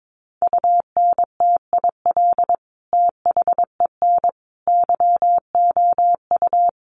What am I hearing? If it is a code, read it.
Morse code, heard as a keyed tone, about 22 wpm, 700 Hz. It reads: UNTIL THEN YOU
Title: Sentences from Top 600 Words 22wpm